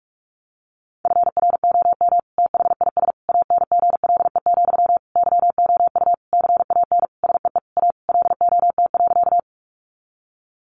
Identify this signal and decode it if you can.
Morse code, heard as a keyed tone, about 44 wpm, 700 Hz. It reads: FROM THIS ANGLE, YOU CAN SEE A LOT.
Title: Sentences from Top 1500 Words 45wpm